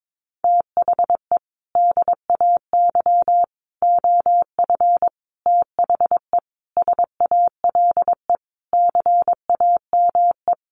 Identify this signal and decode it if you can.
Morse code, heard as a keyed tone, about 22 wpm, 700 Hz. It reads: THE DAY OF THE SALE CAME
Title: Sentences from Top 1400 Words 22wpm